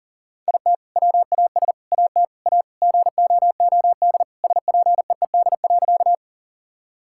Morse code, heard as a keyed tone, about 40 wpm, 700 Hz. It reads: IT WAS AT A GOOD SPEED.
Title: Sentences from Top 1000 Words - Review Part 5 - 40wpm